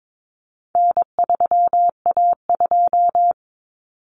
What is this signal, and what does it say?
Morse code, heard as a keyed tone, about 22 wpm, 700 Hz. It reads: N3A2
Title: Common Cut Numbers - 4-Digit 22wpm